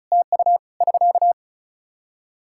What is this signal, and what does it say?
Morse code, heard as a keyed tone, about 34 wpm, 700 Hz. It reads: TU <SK>
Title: Practice QSO - Exchange 4 - Ending - 35wpm